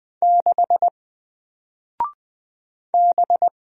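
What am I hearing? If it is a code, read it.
Morse code, heard as a keyed tone, about 20 wpm, 700 Hz. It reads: 6 B
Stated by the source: Single letter and number Morse drill at 20 wpm, from N4LCD.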